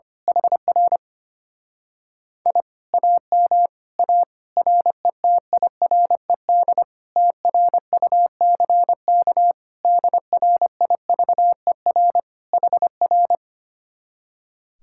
Morse code, heard as a keyed tone, about 24 wpm, 700 Hz. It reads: HR I AM A RETIRED TRUCK DRIVER HR
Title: Speed-Racing Common QSO Phrases - 25wpm